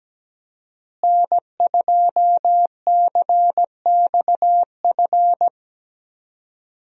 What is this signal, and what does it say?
Morse code, heard as a keyed tone, about 17 wpm, 700 Hz. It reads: N2CXF
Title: USA Call Signs - 5 Characters - Repeated 3x - 17wpm